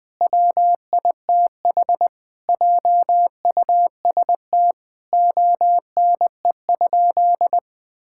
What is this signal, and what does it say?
Morse code, heard as a keyed tone, about 21 wpm, 700 Hz. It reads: WITH JUST ONE?
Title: Sentences from Top 1000 Words - Review Part 6 - 20wpm